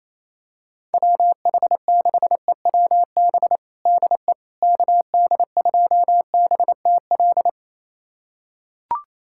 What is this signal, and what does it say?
Morse code, heard as a keyed tone, about 28 wpm, 700 Hz. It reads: WH6EWB DE KD2BTL
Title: USA Callsign DE Callsign - 6 Characters - 28wpm